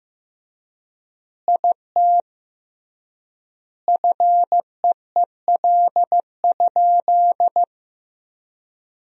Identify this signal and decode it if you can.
Morse code, heard as a keyed tone, about 15 wpm, 700 Hz. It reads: IT FEEL?
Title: 3x Word-Spacing - Sentences from Top 400 Words - 15wpm